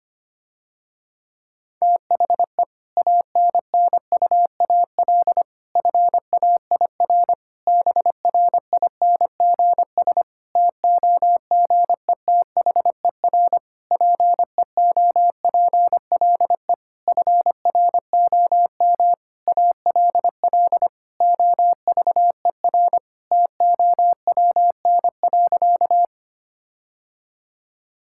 Morse code, heard as a keyed tone, about 25 wpm, 700 Hz. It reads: THE ANNUAL FAIR BRINGS TOGETHER PEOPLE FROM ALL OVER TOWN.
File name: practice / 25wpm / Difficult Sentences with Hints - 25wpm